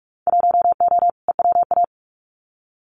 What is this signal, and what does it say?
Morse code, heard as a keyed tone, about 46 wpm, 700 Hz. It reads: 1O EWA
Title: Winter Field Day Exchanges Encore 45wpm